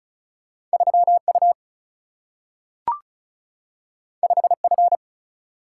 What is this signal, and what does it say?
Morse code, heard as a keyed tone, about 36 wpm, 700 Hz. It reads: 3U 5F
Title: Single-Letter Single-Number 35wpm